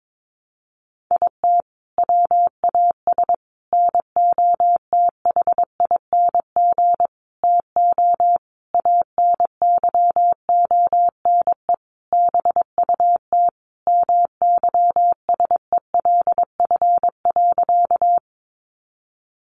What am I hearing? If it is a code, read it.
Morse code, heard as a keyed tone, about 22 wpm, 700 Hz. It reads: IT WAS NOTHING TO ANYONE BUT MYSELF.
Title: Sentences from Top 1300 Words 22wpm